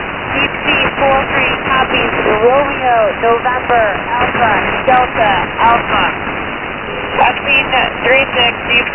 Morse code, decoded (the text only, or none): none